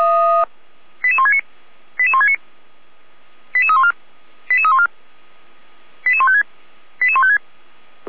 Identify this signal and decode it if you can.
Five-tone selective call (ZVEI), heard as quick runs of five tones, tones of about 70 ms. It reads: 88169 88169 88314 88314 88156 88156
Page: ZVEI